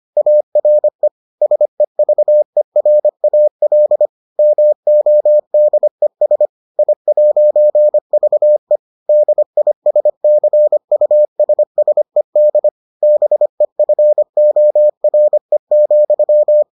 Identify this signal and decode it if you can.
Morse code, heard as a keyed tone, about 25 wpm, 600 Hz. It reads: ARE SEVERAL MODES I'VE DISCUSSED BEFORE,